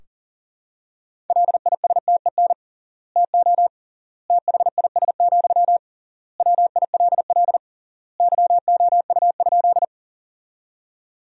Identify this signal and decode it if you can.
Morse code, heard as a keyed tone, about 40 wpm, 700 Hz. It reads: LISTEN TO THIS, WILL YOU?
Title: 3x Word-Spacing - Sentences from Top 400 Words - 40wpm